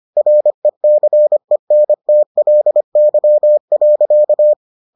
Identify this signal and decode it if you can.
Morse code, heard as a keyed tone, about 25 wpm, 600 Hz. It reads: RECENTLY.